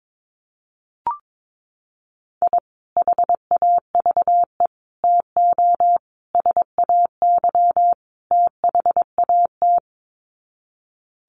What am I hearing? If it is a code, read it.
Morse code, heard as a keyed tone, about 22 wpm, 700 Hz. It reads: I HAVE TO SAY THAT
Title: Sets of 5 Words Encore 22wpm